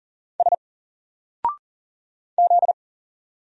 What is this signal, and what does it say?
Morse code, heard as a keyed tone, about 40 wpm, 700 Hz. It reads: S Z
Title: Single Character (Letter:Number:Punctuation:Prosign) Alt 40wpm